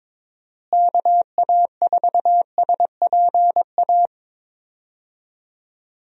Morse code, heard as a keyed tone, about 22 wpm, 700 Hz. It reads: KA4SPA
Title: USA Call Signs - 6 Characters - Repeated 2x - 22wpm